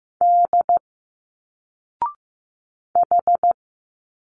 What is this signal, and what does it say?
Morse code, heard as a keyed tone, about 15 wpm, 700 Hz. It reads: D H
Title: Single Letters 15wpm